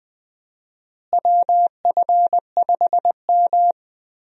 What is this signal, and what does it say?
Morse code, heard as a keyed tone, about 20 wpm, 700 Hz. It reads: WF5M
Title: USA Call Signs - 4 Characters - Repeated 3x - 20wpm